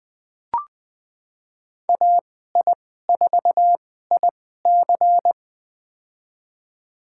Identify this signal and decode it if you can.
Morse code, heard as a keyed tone, about 21 wpm, spaced as for 15 wpm, 700 Hz. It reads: AI4IC
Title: USA Call Signs - 5 Characters - 2x Character Spacing - 20wpm